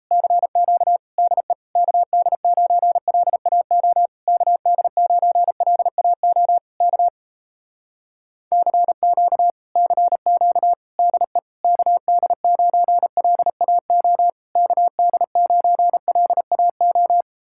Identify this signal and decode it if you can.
Morse code, heard as a keyed tone, about 37 wpm, 700 Hz. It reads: CQ DE KD9LAO KD9LAO K CQ CQ DE KD9LAO KD9LAO